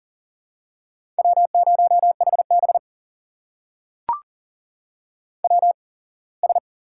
Morse code, heard as a keyed tone, about 40 wpm, spaced as for 24 wpm, 700 Hz. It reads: W0HB WS
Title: USA Call Signs - 8x Character Spacing - 40wpm